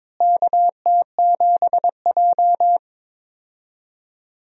Morse code, heard as a keyed tone, about 22 wpm, 700 Hz. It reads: KT7J
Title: USA Call Signs - 4 Characters - Repeated 2x - 22wpm